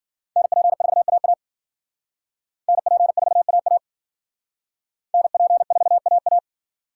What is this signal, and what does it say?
Morse code, heard as a keyed tone, about 50 wpm, 700 Hz. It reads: NP4AA NP4AA NP4AA
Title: Speed Racing - USA 2x2 Call Signs 40wpm